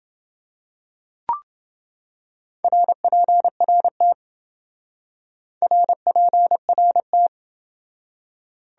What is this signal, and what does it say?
Morse code, heard as a keyed tone, about 27 wpm, 700 Hz. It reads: RPRT RPRT